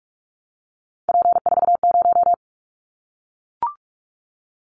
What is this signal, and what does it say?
Morse code, heard as a keyed tone, about 44 wpm, 700 Hz. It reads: P40
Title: Callsign Prefixes for Top 100 DXCC - 45wpm